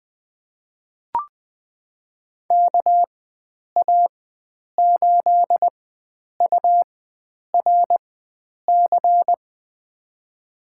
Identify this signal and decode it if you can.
Morse code, heard as a keyed tone, about 21 wpm, spaced as for 9 wpm, 700 Hz. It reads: KA8URC
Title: USA Call Signs - 6 Characters - 4x Character Spacing - 20wpm